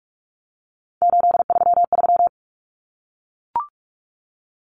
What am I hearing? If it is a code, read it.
Morse code, heard as a keyed tone, about 44 wpm, 700 Hz. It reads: 833
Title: Numbers - 3-Digit 45wpm